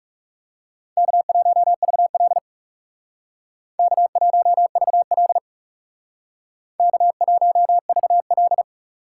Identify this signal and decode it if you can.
Morse code, heard as a keyed tone, about 38 wpm, 700 Hz. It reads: K1VL K1VL K1VL